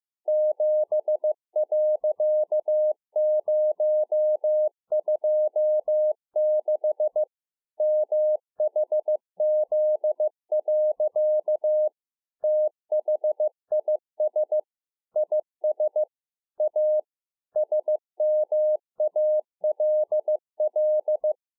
Morse code, heard as a keyed tone, about 15 wpm, 610 Hz. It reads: 7.026 MHZ. THIS IS A SMALL
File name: 2011-04-mp3 / A SMALL TRANCEIVER 2 15WPM